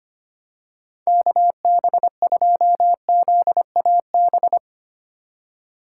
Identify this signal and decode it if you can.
Morse code, heard as a keyed tone, about 25 wpm, 700 Hz. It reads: KB2ZAB
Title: USA Call Signs - 6 Characters - 3x Character Spacing - 25wpm